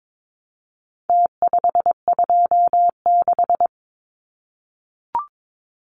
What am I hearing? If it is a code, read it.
Morse code, heard as a keyed tone, about 22 wpm, 700 Hz. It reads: T526
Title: Numbers - Contest Style - 4-Digit 22wpm